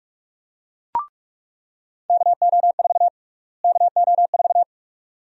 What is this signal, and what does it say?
Morse code, heard as a keyed tone, about 48 wpm, 700 Hz. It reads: KO4 KO4
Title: USA Call Sign Prefixes - Repeated 2x - 45wpm